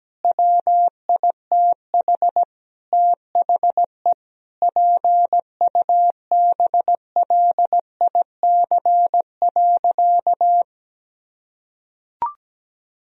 Morse code, heard as a keyed tone, about 17 wpm, 700 Hz. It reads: WITH THE PUBLIC.
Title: Sentences from Top 1000 Words - Review Part 3 - 17wpm